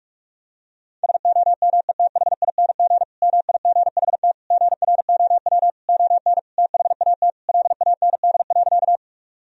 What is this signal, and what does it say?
Morse code, heard as a keyed tone, about 46 wpm, 700 Hz. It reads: SOMETHING MIGHT GROW ON THAT LAND.
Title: Sentences from Top 1000 Words - Review Part 1 - 45wpm